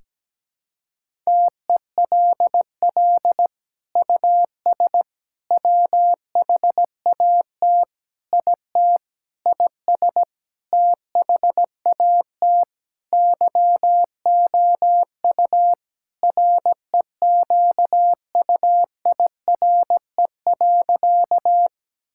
Morse code, heard as a keyed tone, about 17 wpm, 700 Hz. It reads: TELL US WHAT IT IS THAT YOU REQUIRE.
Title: Sentences from Top 1000 Words - Review Part 6 - 17wpm